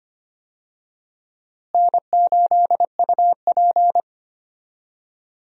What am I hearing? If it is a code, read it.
Morse code, heard as a keyed tone, about 25 wpm, 700 Hz. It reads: N8UP
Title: USA Call Signs - 4 Characters - Repeated 3x - 25wpm